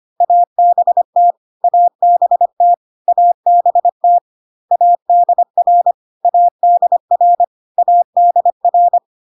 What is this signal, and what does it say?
Morse code, heard as a keyed tone, about 25 wpm, 700 Hz. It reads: ABT ABT ABT ADR ADR ADR
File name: cw / CW common abbreviations alphabetical repeated 3 times 25 25wpm ik1hge